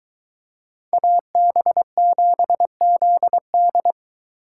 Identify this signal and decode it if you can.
Morse code, heard as a keyed tone, about 23 wpm, 700 Hz. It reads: AB7ZD